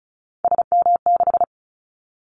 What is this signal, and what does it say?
Morse code, heard as a keyed tone, about 36 wpm, 700 Hz. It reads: SM6
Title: Callsign Prefixes for Top 25 DXCC - 35wpm